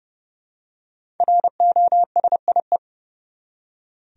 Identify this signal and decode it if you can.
Morse code, heard as a keyed tone, about 31 wpm, 700 Hz. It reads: ROSIE
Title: USA Common Names - 30wpm